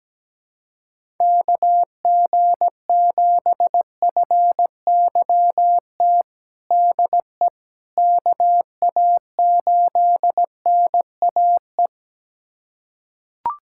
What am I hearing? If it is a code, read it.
Morse code, heard as a keyed tone, about 17 wpm, 700 Hz. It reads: KG7FYT DE KA8NAE
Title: USA Callsign DE Callsign - 6 Characters - 17wpm